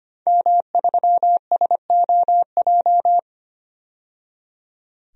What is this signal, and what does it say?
Morse code, heard as a keyed tone, about 25 wpm, 700 Hz. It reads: M3SOJ
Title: DXCC-100 Call Signs - 5 Characters - 25wpm